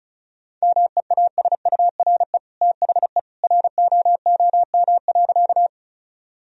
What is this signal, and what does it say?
Morse code, heard as a keyed tone, about 36 wpm, 700 Hz. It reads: MEASURE THE ROOM.
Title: Sentences from Top 1000 Words - Review Part 2 - 35wpm